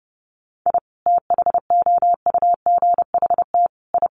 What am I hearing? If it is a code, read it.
Morse code, heard as a keyed tone, about 30 wpm, 700 Hz. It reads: I THOUGHT I
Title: Sets of 5 Words Encore2 30wpm